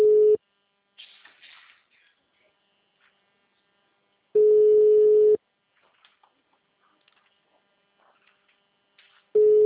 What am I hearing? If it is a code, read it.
Ringback tone (Europe, 425 Hz).
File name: out- -2004- - - .7883